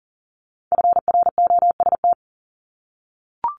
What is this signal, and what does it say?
Morse code, heard as a keyed tone, about 39 wpm, 700 Hz. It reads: FROST
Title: Multi-letter Phonemes 40wpm